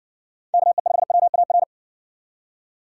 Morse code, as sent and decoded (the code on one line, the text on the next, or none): -.- . ..... .-- .- .-.
KE5WAR